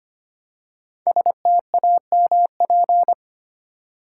Morse code, heard as a keyed tone, about 25 wpm, 700 Hz. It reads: STAMP
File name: 5 Letter Words - First 3 Letters Given - 25wpm